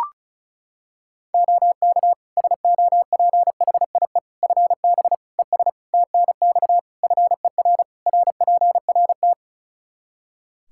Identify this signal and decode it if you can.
Morse code, heard as a keyed tone, about 34 wpm, 700 Hz. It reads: OK SOPHIE FB ES TNX FER RPRT
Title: Practice QSO - Exchange 2 - Info - 35wpm